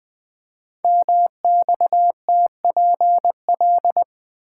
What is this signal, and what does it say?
Morse code, heard as a keyed tone, about 20 wpm, 700 Hz. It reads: MXTPL
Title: Code Groups - 5 Letters - 20wpm